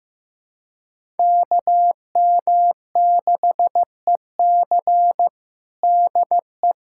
Morse code, heard as a keyed tone, about 15 wpm, 700 Hz. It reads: KM6EC DE
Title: USA Callsign DE Callsign - 5 Characters - 15wpm